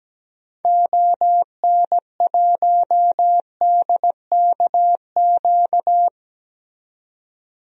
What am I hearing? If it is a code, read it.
Morse code, heard as a keyed tone, about 17 wpm, 700 Hz. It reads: ON1DKQ
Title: DXCC-100 Call Signs - 6 Characters - 17wpm